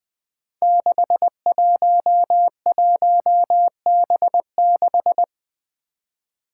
Morse code, heard as a keyed tone, about 20 wpm, 700 Hz. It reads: 611B6
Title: Five Challenging Letter-Number Combinations 20wpm